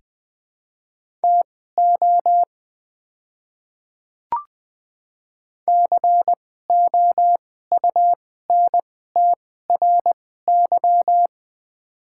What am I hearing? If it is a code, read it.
Morse code, heard as a keyed tone, about 20 wpm, spaced as for 14 wpm, 700 Hz. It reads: TO COUNTRY
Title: 2x Character Spacing - Top 500 Words - 20wpm